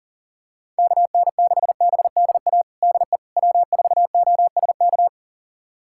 Morse code, heard as a keyed tone, about 38 wpm, 700 Hz. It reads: KN6BDA DE W4OSK
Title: USA Callsign DE Callsign 40wpm